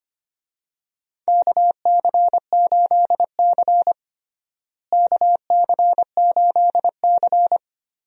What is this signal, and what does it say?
Morse code, heard as a keyed tone, about 25 wpm, 700 Hz. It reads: KC8C KC8C
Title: USA Call Signs - 4 Characters - Repeated 3x - 25wpm